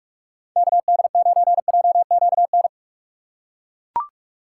Morse code, heard as a keyed tone, about 41 wpm, 700 Hz. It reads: KD9JQN
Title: USA Call Signs - 6 Characters - 3x Character Spacing - 45wpm